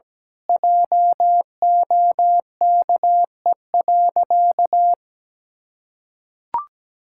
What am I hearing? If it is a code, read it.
Morse code, heard as a keyed tone, about 17 wpm, 700 Hz. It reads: JOKE.
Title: Pangram Challenge v2 - 17wpm